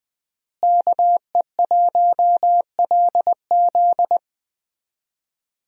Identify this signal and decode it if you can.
Morse code, heard as a keyed tone, about 21 wpm, 700 Hz. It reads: KE1LZ
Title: USA Call Signs - 5 Characters - Repeated 2x - 20wpm